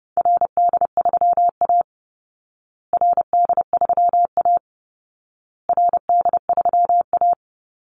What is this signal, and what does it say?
Morse code, heard as a keyed tone, about 30 wpm, 700 Hz. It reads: RD3A RD3A RD3A
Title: Call Signs Repeated 2x - 30wpm